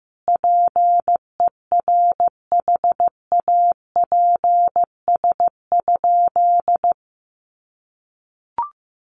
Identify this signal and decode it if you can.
Morse code, heard as a keyed tone, about 15 wpm, 700 Hz. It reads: PERHAPS?
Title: Sentences from Top 1200 Words 15wpm